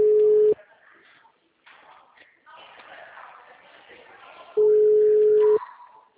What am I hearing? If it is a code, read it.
Ringback tone (Europe, 425 Hz).
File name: out- -2006- - - .14577